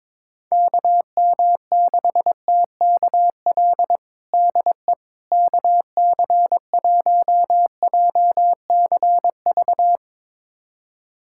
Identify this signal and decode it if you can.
Morse code, heard as a keyed tone, about 22 wpm, 700 Hz. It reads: KM6TKL DE KC1JCV
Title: USA Callsign DE Callsign 22wpm